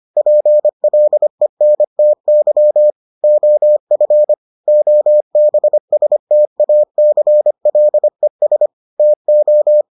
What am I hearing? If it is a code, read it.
Morse code, heard as a keyed tone, about 25 wpm, 600 Hz. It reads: PLENTY OF OBSTACLES TO